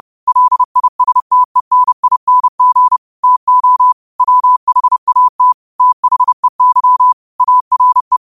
Morse code, heard as a keyed tone, about 30 wpm, 1000 Hz. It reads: LISTENING TO WHAT THEY ARE